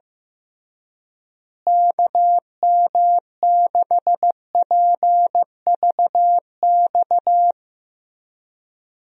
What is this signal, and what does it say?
Morse code, heard as a keyed tone, about 15 wpm, 700 Hz. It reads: KM6PVX
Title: USA Call Signs - 6 Characters - Repeated 3x - 15wpm